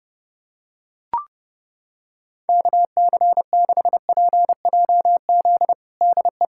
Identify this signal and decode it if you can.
Morse code, heard as a keyed tone, about 30 wpm, 700 Hz. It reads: KC6PJZ DE
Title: USA Callsign DE Callsign 30wpm